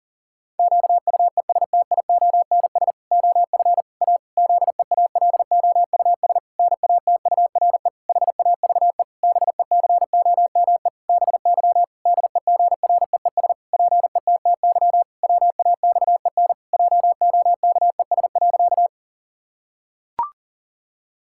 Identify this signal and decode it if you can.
Morse code, heard as a keyed tone, about 40 wpm, 700 Hz. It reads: QUESTIONS OF A ZEALOUS NATURE HAVE BECOME BY DEGREES PETTY WAXEN JOKES.
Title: Pangram Challenge v2 - 40wpm